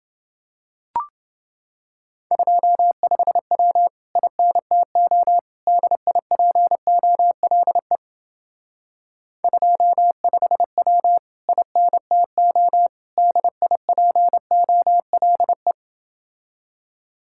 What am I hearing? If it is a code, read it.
Morse code, heard as a keyed tone, about 27 wpm, 700 Hz. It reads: 25W INTO DIPOLE 25W INTO DIPOLE